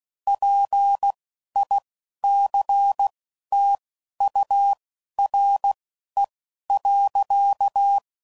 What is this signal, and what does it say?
Morse code, heard as a keyed tone, about 16 wpm, spaced as for 11 wpm, 800 Hz. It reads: PICTURE.
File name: CW Code Groups @10WPM / Technical Subjects / AND Gate